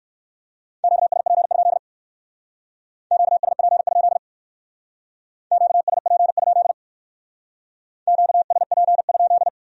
Morse code, heard as a keyed tone, about 55 wpm, 700 Hz. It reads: QSP? QSP? QSP? QSP?
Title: Speed-Racing Q-Codes - 45wpm